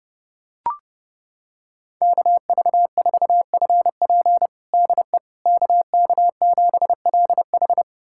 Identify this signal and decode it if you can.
Morse code, heard as a keyed tone, about 30 wpm, 700 Hz. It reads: KV4FP DE KK7LH
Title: USA Callsign DE Callsign - 5 Characters - 30wpm